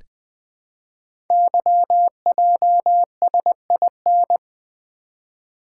Morse code, heard as a keyed tone, about 20 wpm, 700 Hz. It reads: YJSIN
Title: Code Groups - 5 Letters - 20wpm